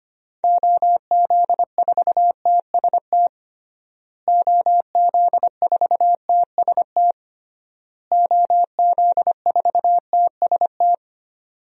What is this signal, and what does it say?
Morse code, heard as a keyed tone, about 25 wpm, 700 Hz. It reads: OZ4TST OZ4TST OZ4TST
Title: DXCC-100 Call Signs - 6 Characters - Repeated 3x - 25wpm